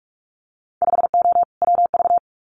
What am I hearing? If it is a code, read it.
Morse code, heard as a keyed tone, about 46 wpm, 700 Hz. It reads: 5O WV
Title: Winter Field Day Exchanges Encore 45wpm